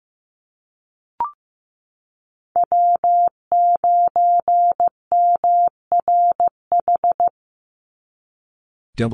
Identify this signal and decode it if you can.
Morse code, heard as a keyed tone, about 15 wpm, 700 Hz. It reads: W9MRH
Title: NA SOTA Chaser Callsigns 15wpm